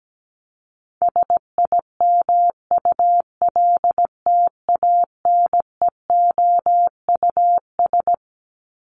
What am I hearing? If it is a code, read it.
Morse code, heard as a keyed tone, about 17 wpm, 700 Hz. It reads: SIMULTANEOUS
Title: LY Suffix - Top 100 Words with Base Words 17wpm